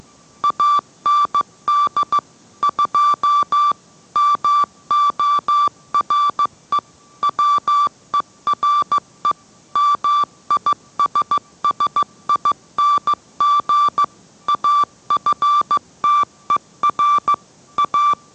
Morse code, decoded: AND 2 MORE WERE MISSING AFTER A

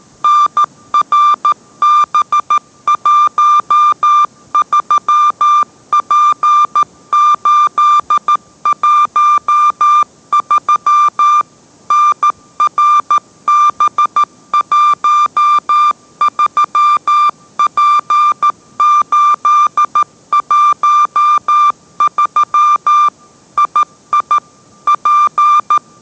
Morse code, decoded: NRB13P813 NRB13P813 II P

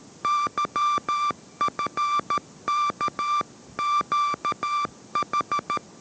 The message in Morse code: YFKQH

